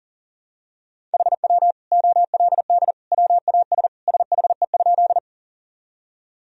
Morse code, heard as a keyed tone, about 40 wpm, 700 Hz. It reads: HW OLD WAS SHE?
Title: Sentences from Top 200 Words and Abbreviations 40wpm